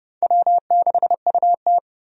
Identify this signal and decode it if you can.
Morse code, heard as a keyed tone, about 30 wpm, 700 Hz. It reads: W6UT
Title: USA Call Signs - 4 Characters - Repeated 2x - 30wpm